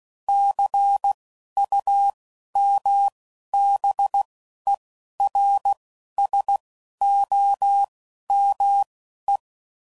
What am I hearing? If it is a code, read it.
Morse code, heard as a keyed tone, about 16 wpm, spaced as for 11 wpm, 800 Hz. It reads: CUMBERSOME
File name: CW Code Groups @10WPM/Technical Subjects